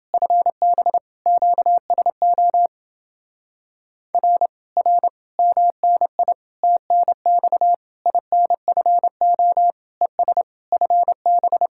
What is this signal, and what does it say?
Morse code, heard as a keyed tone, about 29 wpm, 700 Hz. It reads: FB QSO R R MNI TNX INFO ES FB